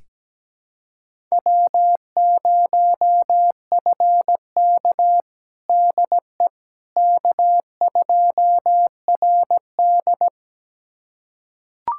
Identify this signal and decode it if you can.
Morse code, heard as a keyed tone, about 17 wpm, 700 Hz. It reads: W0FK DE K2RD
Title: USA Callsign DE Callsign - 4 Characters - 17wpm